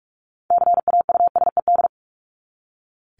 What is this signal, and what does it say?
Morse code, heard as a keyed tone, about 44 wpm, 700 Hz. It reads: CAUSED
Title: ED Suffix - Top 100 Words with Base Words 45wpm